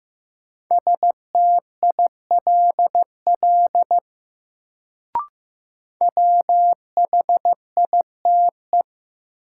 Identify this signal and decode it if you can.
Morse code, heard as a keyed tone, about 16 wpm, 700 Hz. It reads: STILL WHITE
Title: Sentences from Top 300 Words - New Word Drill - 15wpm